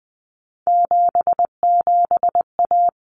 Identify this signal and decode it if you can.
Morse code, heard as a keyed tone, about 20 wpm, 700 Hz. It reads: 77A
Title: Common Cut Numbers - 3-Digit 20wpm